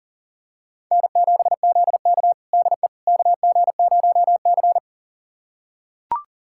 Morse code, heard as a keyed tone, about 40 wpm, 700 Hz. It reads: N7ZK DE KG0C
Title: USA Callsign DE Callsign - 4 Characters - 40wpm